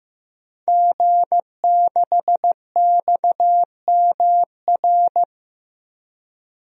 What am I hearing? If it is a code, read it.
Morse code, heard as a keyed tone, about 15 wpm, 700 Hz. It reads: G6XMR
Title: DXCC-100 Call Signs - 5 Characters - Repeated 2x - 15wpm